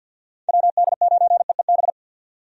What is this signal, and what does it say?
Morse code, heard as a keyed tone, about 49 wpm, 700 Hz. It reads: WD9EEB
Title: USA Call Signs - 6 Characters - 50wpm